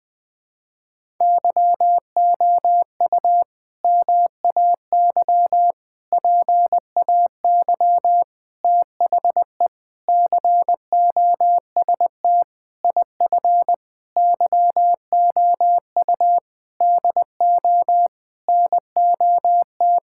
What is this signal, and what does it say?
Morse code, heard as a keyed tone, about 21 wpm, 700 Hz. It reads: YOU MAY PAY THE COST IF YOU DO NOT
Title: Sentences with 8-Word Phrases - 20wpm